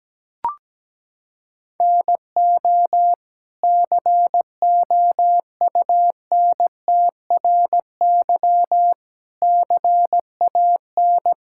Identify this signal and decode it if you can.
Morse code, heard as a keyed tone, about 17 wpm, 700 Hz. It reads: NO COUNTRY CAN
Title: Sentences from Top 1000 Words - Review Part 5 - 17wpm